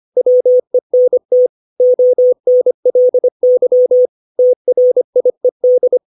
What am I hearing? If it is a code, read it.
Morse code, heard as a keyed tone, about 25 wpm, 500 Hz. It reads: WENT ONLY TRIED
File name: 632 words / 632 Words 25WPM